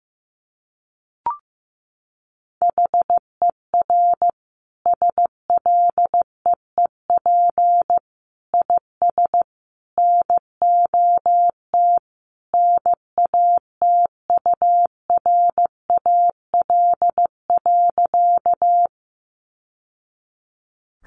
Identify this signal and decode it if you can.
Morse code, heard as a keyed tone, about 15 wpm, 700 Hz. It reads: HER SLEEP IS NOT NATURAL.
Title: Sentences from Top 700 Words 15wpm